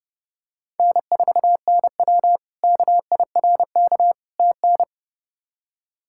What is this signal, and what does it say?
Morse code, heard as a keyed tone, about 30 wpm, 700 Hz. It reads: N4NW KIRK TN
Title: K1USN SST QSO Practice - 30wpm